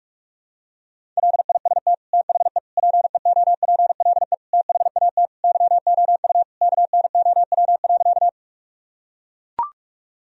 Morse code, heard as a keyed tone, about 42 wpm, 700 Hz. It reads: LIST THE PEOPLE THAT YOU KNOW.
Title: Sentences from Top 1000 Words - Review Part 2 - 45wpm